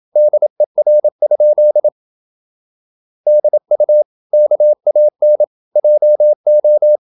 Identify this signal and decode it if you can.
Morse code, heard as a keyed tone, about 27 wpm, 600 Hz. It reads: DER? DU KAN JO
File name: Ferie QRQ 27wpm